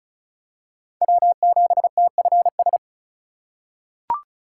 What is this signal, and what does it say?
Morse code, heard as a keyed tone, about 36 wpm, 700 Hz. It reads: W7TFS
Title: USA Call Signs - 5 Characters - 2x Character Spacing - 35wpm